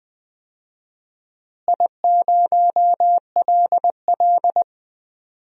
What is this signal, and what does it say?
Morse code, heard as a keyed tone, about 20 wpm, 700 Hz. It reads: I0LL
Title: DXCC-100 Call Signs - 4 Characters - Repeated 2x - 20wpm